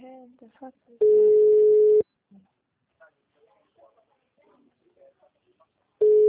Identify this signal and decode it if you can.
Ringback tone (Europe, 425 Hz), with speech.